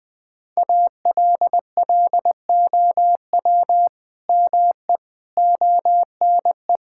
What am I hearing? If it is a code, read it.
Morse code, heard as a keyed tone, about 20 wpm, 700 Hz. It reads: ALLOW ME ONE
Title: Sentences from Top 1000 Words - Review Part 5 - 20wpm